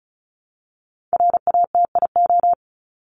Morse code, heard as a keyed tone, about 34 wpm, 700 Hz. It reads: RATIO
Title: Top 1751-2000 Words 35wpm